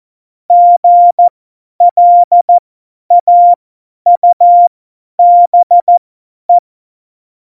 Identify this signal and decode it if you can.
Morse code, heard as a keyed tone, about 14 wpm, spaced as for 10 wpm, 700 Hz. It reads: GLAUBE
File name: Morsekurs / Witze / Witz 23. Januar 2018, 50 F70